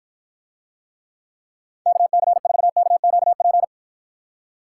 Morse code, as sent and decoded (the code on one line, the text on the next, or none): -.- -.-. ....- -.- --.- .--.
KC4KQP